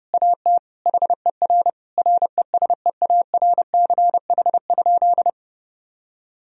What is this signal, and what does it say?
Morse code, heard as a keyed tone, about 30 wpm, 700 Hz. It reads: AT HER RESEARCH?
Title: Sentences from Top 1000 Words - Review Part 4 - 30wpm